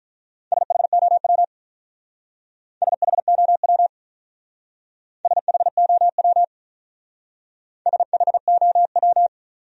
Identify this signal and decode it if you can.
Morse code, heard as a keyed tone, about 45 wpm, 700 Hz. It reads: SHOW SHOW SHOW SHOW